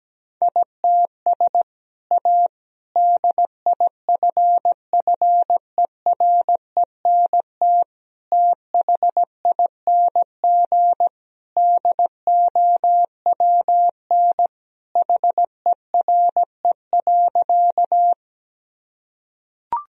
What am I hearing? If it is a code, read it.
Morse code, heard as a keyed tone, about 17 wpm, 700 Hz. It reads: ITS A DIFFERENT THING DOWN HERE.
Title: Sentences from Top 1000 Words - Review Part 3 - 17wpm